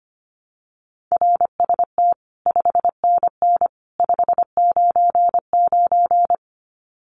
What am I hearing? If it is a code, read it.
Morse code, heard as a keyed tone, about 25 wpm, 700 Hz. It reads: RST 5NN 599
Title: Common QSO Phrases 25wpm